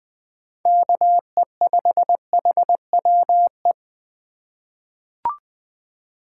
Morse code, heard as a keyed tone, about 20 wpm, 700 Hz. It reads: KE5HWE
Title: USA Call Signs - 6 Characters - 2x Character Spacing - 20wpm